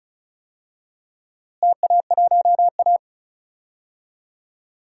Morse code, heard as a keyed tone, about 36 wpm, 700 Hz. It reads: TA1A
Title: DXCC-100 Call Signs - 4 Characters - Repeated 2x - 35wpm